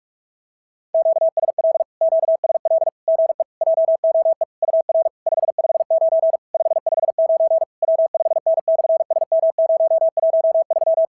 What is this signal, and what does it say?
Morse code, heard as a keyed tone, about 46 wpm, 650 Hz. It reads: QSL QSL GE JOE UR 559 559 W5N/SM013
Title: SOTA QSO Practice 45wpm